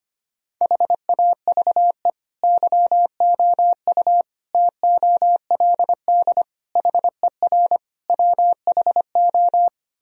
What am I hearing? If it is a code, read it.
Morse code, heard as a keyed tone, about 25 wpm, 700 Hz. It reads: HAVE YOU TOLD HER WHO
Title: Sentences from Top 1000 Words - Review Part 2 - 25wpm